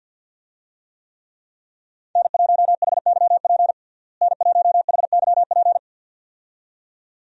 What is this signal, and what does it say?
Morse code, heard as a keyed tone, about 52 wpm, 700 Hz. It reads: N1HYP N1HYP
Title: USA Call Signs - 5 Characters - Repeated 2x - 50wpm